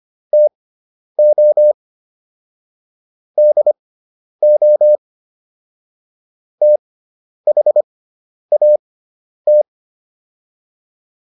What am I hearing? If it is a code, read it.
Morse code, heard as a keyed tone, about 25 wpm, spaced as for 10 wpm, 600 Hz. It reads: TO DO THAT